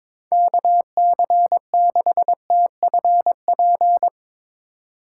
Morse code, heard as a keyed tone, about 22 wpm, 700 Hz. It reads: KC6TFP
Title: USA Call Signs - 6 Characters - 2x Character Spacing - 22wpm